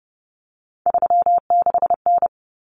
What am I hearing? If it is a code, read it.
Morse code, heard as a keyed tone, about 30 wpm, 700 Hz. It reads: 36N
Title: Common Cut Numbers - 3-Digit 30wpm